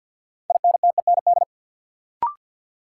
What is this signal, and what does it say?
Morse code, heard as a keyed tone, about 53 wpm, 700 Hz. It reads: INTEND
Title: Sentences from Top 1600 Words - New Word Drill - 50wpm